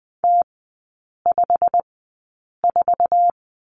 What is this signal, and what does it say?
Morse code, heard as a keyed tone, about 20 wpm, spaced as for 8 wpm, 700 Hz. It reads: T54
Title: 121 Lesson 20e - Number 3 - Three Characters 20wpm